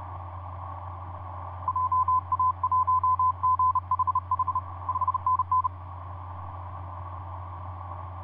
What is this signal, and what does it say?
Morse code, heard as a keyed tone, about 32 wpm, 1020 Hz. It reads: JA1GHH 5NN
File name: vp8dmn ja1ghh etc 30m cw 23jan2010 2351